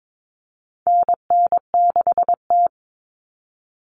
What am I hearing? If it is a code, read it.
Morse code, heard as a keyed tone, about 22 wpm, 700 Hz. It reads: NN6T
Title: Call Signs - 22wpm